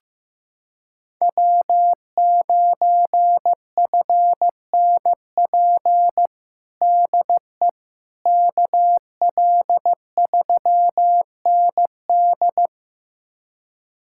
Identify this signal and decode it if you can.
Morse code, heard as a keyed tone, about 15 wpm, 700 Hz. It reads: W9FNP DE KL3ND
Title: USA Callsign DE Callsign - 5 Characters - 15wpm